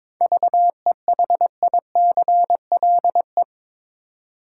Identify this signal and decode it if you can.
Morse code, heard as a keyed tone, about 22 wpm, 700 Hz. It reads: VEHICLE
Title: Seven Letter Words - 22wpm